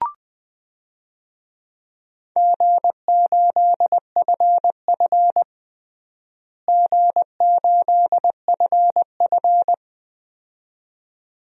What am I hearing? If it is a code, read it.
Morse code, heard as a keyed tone, about 20 wpm, 700 Hz. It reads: G8FF G8FF
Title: DXCC-100 Call Signs - 4 Characters - Repeated 2x - 20wpm